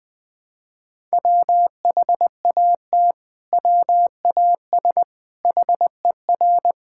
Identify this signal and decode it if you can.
Morse code, heard as a keyed tone, about 20 wpm, 700 Hz. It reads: WHAT WAS HER
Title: Sentences from Top 1000 Words - Review Part 5 - 20wpm